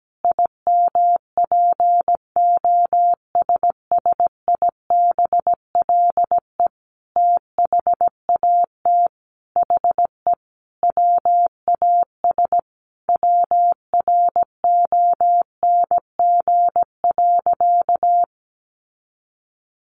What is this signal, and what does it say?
Morse code, heard as a keyed tone, about 17 wpm, 700 Hz. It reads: IMPOSSIBLE THAT HE WAS WRONG.
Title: Sentences from Top 1100 Words 17wpm